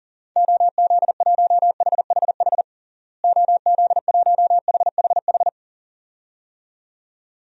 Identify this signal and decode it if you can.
Morse code, heard as a keyed tone, about 37 wpm, 700 Hz. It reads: OZ1HHH OZ1HHH
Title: DXCC-100 Call Signs - 6 Characters - Repeated 2x - 40wpm